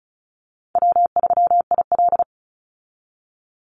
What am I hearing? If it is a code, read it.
Morse code, heard as a keyed tone, about 34 wpm, 700 Hz. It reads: W3IL
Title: Easy Call Signs Repeated 3x - 35wpm